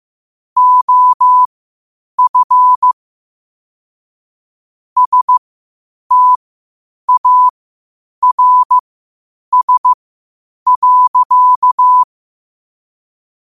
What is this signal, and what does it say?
Morse code, heard as a keyed tone, about 15 wpm, spaced as for 8 wpm, 1000 Hz. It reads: OF STARS.
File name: qotdfiles / 8wpm